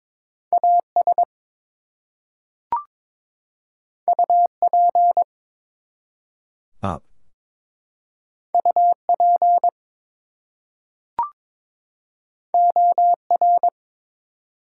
Morse code, heard as a keyed tone, about 22 wpm, 700 Hz. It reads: AS UP UP OR
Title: Two Letter Words - 22wpm